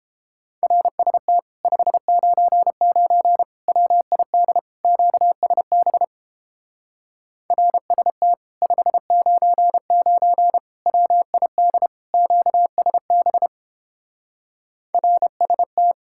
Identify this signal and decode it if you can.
Morse code, heard as a keyed tone, about 29 wpm, 700 Hz. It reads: RST 599 WID QSB RST 599 WID QSB RST